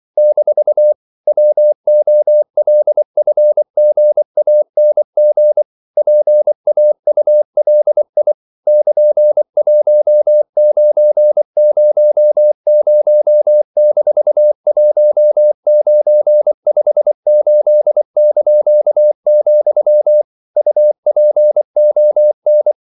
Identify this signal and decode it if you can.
Morse code, heard as a keyed tone, about 24 wpm, 600 Hz. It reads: - WOLFGANG PAULI <KN>1900-1958), UPON
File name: CW / Practice / EnglishQuotations0424